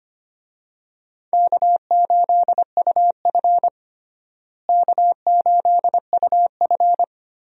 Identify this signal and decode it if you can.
Morse code, heard as a keyed tone, about 25 wpm, 700 Hz. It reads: K8UF K8UF
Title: USA Call Signs - 4 Characters - Repeated 2x - 25wpm